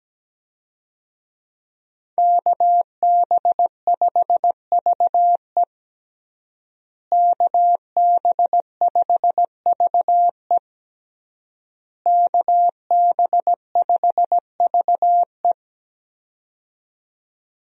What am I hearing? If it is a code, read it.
Morse code, heard as a keyed tone, about 17 wpm, 700 Hz. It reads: KB5VE KB5VE KB5VE
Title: USA Call Signs - 5 Characters - Repeated 3x - 17wpm